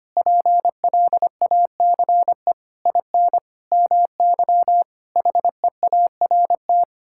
Morse code, heard as a keyed tone, about 25 wpm, 700 Hz. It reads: PLACE IN MY HEART
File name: Sets of 9 Words - 25wpm